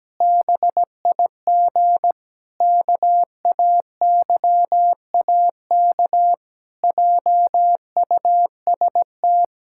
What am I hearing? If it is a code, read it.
Morse code, heard as a keyed tone, about 17 wpm, 700 Hz. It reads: BIG KAYAK JUST
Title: Pangram Challenge v2 - 17wpm